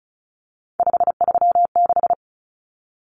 Morse code, heard as a keyed tone, about 34 wpm, 700 Hz. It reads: 536
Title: Numbers - 3-Digit 35wpm